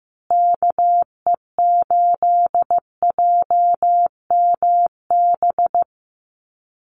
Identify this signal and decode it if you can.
Morse code, heard as a keyed tone, about 15 wpm, 700 Hz. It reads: KE8JMB
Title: USA Call Signs Repeated 3x 15wpm